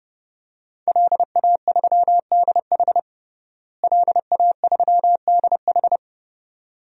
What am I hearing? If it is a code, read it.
Morse code, heard as a keyed tone, about 32 wpm, 700 Hz. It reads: LA3DH LA3DH
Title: DXCC-100 Call Signs - 5 Characters - Repeated 2x - 30wpm